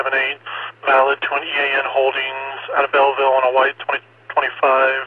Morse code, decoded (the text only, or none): none